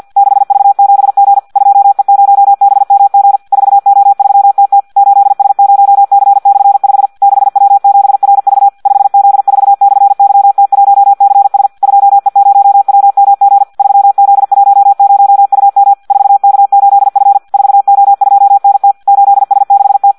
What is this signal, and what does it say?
Morse code, heard as a keyed tone, about 50 wpm, 800 Hz. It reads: 6V7G 2E0BMG 4O3TT 8S0C/5 6W7RV 5Z4/YT1CS 2E0WMG 3Z10UM 4K8F 4O2NT 8S6T
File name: Audio practica CW / qrq-tx-50 2